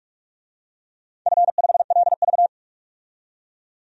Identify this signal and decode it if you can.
Morse code, heard as a keyed tone, about 44 wpm, 700 Hz. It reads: F5LV
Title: DXCC-100 Call Signs - 4 Characters - 45wpm